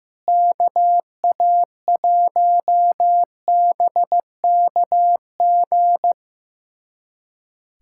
Morse code, heard as a keyed tone, about 15 wpm, 700 Hz. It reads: KA1BKG
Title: USA Callsign DE Callsign - 6 Characters - 15wpm